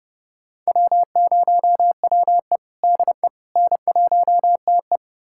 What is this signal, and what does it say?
Morse code, heard as a keyed tone, about 31 wpm, 700 Hz. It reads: W0WE DE N1TE
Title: USA Callsign DE Callsign - 4 Characters - 30wpm